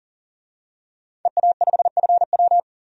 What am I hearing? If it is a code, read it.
Morse code, heard as a keyed tone, about 40 wpm, 700 Hz. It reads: EA5FW
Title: DXCC-100 Call Signs - 5 Characters - 40wpm